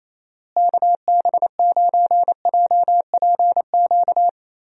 Morse code, heard as a keyed tone, about 28 wpm, 700 Hz. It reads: KB9JPQ
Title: USA Call Signs - 6 Characters - Repeated 3x - 28wpm